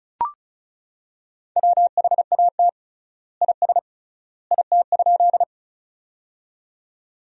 Morse code, heard as a keyed tone, about 36 wpm, 700 Hz. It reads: WHAT IS IT?
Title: 3x Word-Spacing - Sentences from Top 100 Words - 35wpm